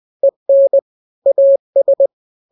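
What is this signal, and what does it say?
Morse code, heard as a keyed tone, about 20 wpm, 550 Hz. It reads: EN AS
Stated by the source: Morse drill of two-letter n-grams at 20/18 WPM (Farnsworth), from KK7RR.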